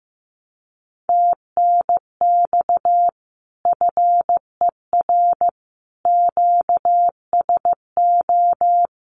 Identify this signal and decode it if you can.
Morse code, heard as a keyed tone, about 15 wpm, 700 Hz. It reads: TNX FER QSO
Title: Common QSO Phrases Encore 15wpm